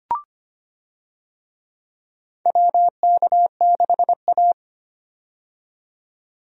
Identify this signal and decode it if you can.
Morse code, heard as a keyed tone, about 25 wpm, 700 Hz. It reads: WK6A
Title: USA Call Signs - 4 Characters - 25wpm